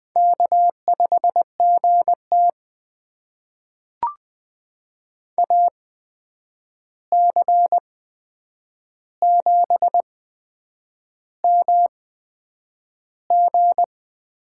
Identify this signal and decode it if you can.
Morse code, heard as a keyed tone, about 20 wpm, spaced as for 8 wpm, 700 Hz. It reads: K5GT AC7MG
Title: USA Call Signs - 8x Character Spacing - 20wpm